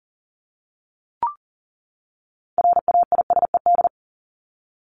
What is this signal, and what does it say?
Morse code, heard as a keyed tone, about 40 wpm, 700 Hz. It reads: RAISED
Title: ED Suffix - Top 100 Words with Base Words 40wpm